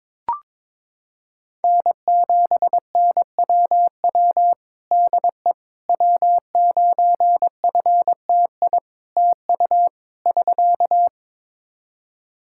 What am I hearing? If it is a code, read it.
Morse code, heard as a keyed tone, about 22 wpm, 700 Hz. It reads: N7NWW DE W9FTI TU <SK>
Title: Practice QSO - Exchange 4 - Ending - 22wpm